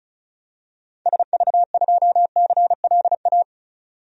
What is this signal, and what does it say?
Morse code, heard as a keyed tone, about 36 wpm, 700 Hz. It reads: SV2CLA
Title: DXCC-100 Call Signs - 6 Characters - Repeated 2x - 35wpm